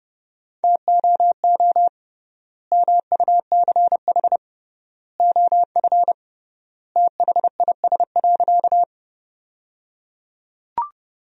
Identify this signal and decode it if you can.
Morse code, heard as a keyed tone, about 30 wpm, 700 Hz. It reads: TOO MUCH OF THIS.
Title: 3x Word-Spacing - Sentences from Top 200 Words - 30wpm